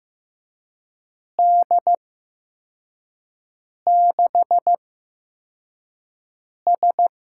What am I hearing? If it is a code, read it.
Morse code, heard as a keyed tone, about 15 wpm, spaced as for 4 wpm, 700 Hz. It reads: D6S
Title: USA Call Signs - 6 Characters - 8x Character Spacing - 15wpm